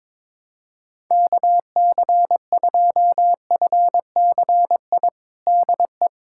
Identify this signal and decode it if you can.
Morse code, heard as a keyed tone, about 22 wpm, 700 Hz. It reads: KC2FCI DE
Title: USA Callsign DE Callsign 22wpm